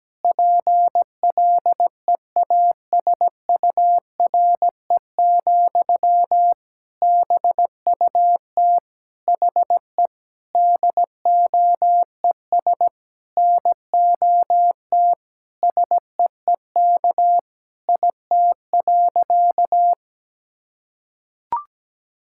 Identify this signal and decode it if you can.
Morse code, heard as a keyed tone, about 17 wpm, 700 Hz. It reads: PLEASURE, BUT HE DOES NOT SEEK IT.
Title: Sentences from Top 1600 Words 17wpm